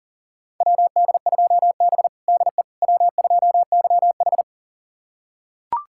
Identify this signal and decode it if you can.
Morse code, heard as a keyed tone, about 40 wpm, 690 Hz. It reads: WD2B DE W2YH
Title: USA Callsign DE Callsign - 4 Characters - 40wpm